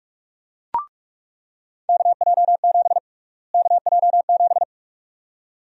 Morse code, heard as a keyed tone, about 44 wpm, 700 Hz. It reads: KJ7 KJ7
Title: USA Call Sign Prefixes - Repeated 2x - 45wpm